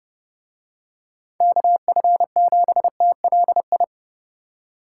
Morse code, heard as a keyed tone, about 31 wpm, 700 Hz. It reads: KF7TLI